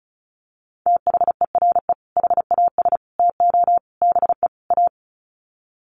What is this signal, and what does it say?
Morse code, heard as a keyed tone, about 34 wpm, 700 Hz. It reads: THERE HAS TO BE A
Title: Sets of 5 Words Encore 35wpm